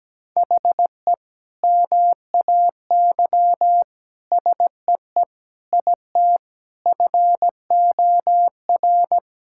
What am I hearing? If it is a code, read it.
Morse code, heard as a keyed tone, about 17 wpm, 700 Hz. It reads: HE MAY SEE IT FOR
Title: Sentences from Top 1600 Words 17wpm